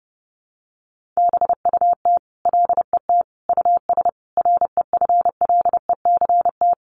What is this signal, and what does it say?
Morse code, heard as a keyed tone, about 30 wpm, 700 Hz. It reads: BUT LET US REFLECT
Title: Sentences from Top 1500 Words 30wpm